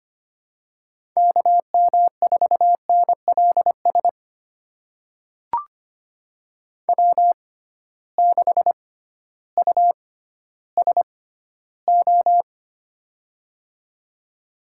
Morse code, heard as a keyed tone, about 25 wpm, spaced as for 13 wpm, 700 Hz. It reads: KM4NLS W6USO
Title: USA Call Signs - 6x Character Spacing - 25wpm